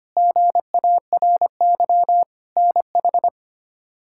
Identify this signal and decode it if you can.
Morse code, heard as a keyed tone, about 25 wpm, 700 Hz. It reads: GARY NH
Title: K1USN SST QSO Practice - 25wpm